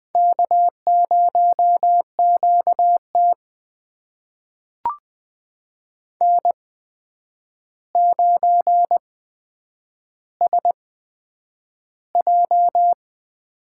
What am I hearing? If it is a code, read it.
Morse code, heard as a keyed tone, about 20 wpm, spaced as for 9 wpm, 700 Hz. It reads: K0QT N9SJ
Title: USA Call Signs - 4 Characters - 8x Character Spacing - 20wpm